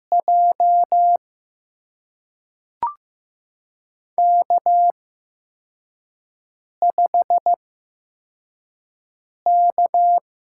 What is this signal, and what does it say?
Morse code, heard as a keyed tone, about 15 wpm, spaced as for 4 wpm, 700 Hz. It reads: J K5K
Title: USA Call Signs - 5 Characters - 8x Character Spacing - 15wpm